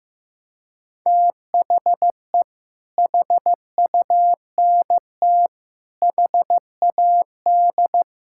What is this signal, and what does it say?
Morse code, heard as a keyed tone, about 15 wpm, 700 Hz. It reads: THE HUNT HAD
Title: Sentences from Top 1600 Words 15wpm